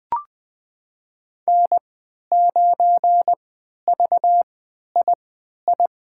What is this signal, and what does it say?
Morse code, heard as a keyed tone, about 20 wpm, spaced as for 11 wpm, 700 Hz. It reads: N9VII
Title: USA Call Signs - 5 Characters - 3x Character Spacing - 20wpm